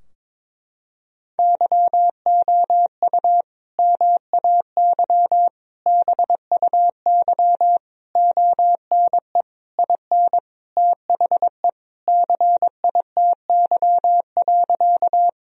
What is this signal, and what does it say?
Morse code, heard as a keyed tone, about 22 wpm, 700 Hz. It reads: YOU MAY BUY ONE IN THE CITY.
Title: Sentences from Top 1000 Words - Review Part 4 - 22wpm